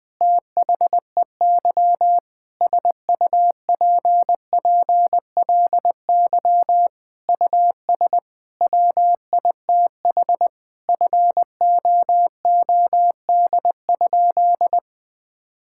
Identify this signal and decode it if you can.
Morse code, heard as a keyed tone, about 20 wpm, 700 Hz. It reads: THEY SUPPLY US WITH FOOD?
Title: Sentences from Top 1000 Words - Review Part 5 - 20wpm